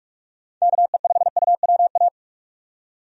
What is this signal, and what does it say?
Morse code, heard as a keyed tone, about 48 wpm, 700 Hz. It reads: KE5UWA
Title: USA Call Signs - 6 Characters - 45wpm